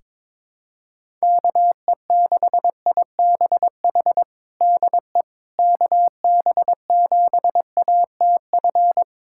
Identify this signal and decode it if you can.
Morse code, heard as a keyed tone, about 22 wpm, 700 Hz. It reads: KE6IBH DE KB7ATF
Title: USA Callsign DE Callsign - 6 Characters - 22wpm